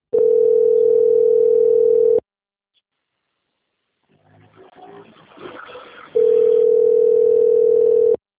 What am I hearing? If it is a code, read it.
Ringback tone (North America), 440 + 480 Hz.